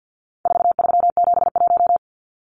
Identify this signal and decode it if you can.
Morse code, heard as a keyed tone, about 50 wpm, 700 Hz. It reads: 4371
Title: Numbers - 4-Digit 50wpm